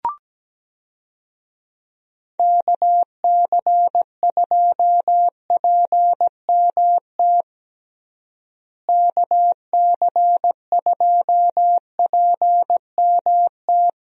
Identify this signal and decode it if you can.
Morse code, heard as a keyed tone, about 17 wpm, 700 Hz. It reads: KC2PMT KC2PMT
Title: USA Call Signs - 6 Characters - Repeated 2x - 17wpm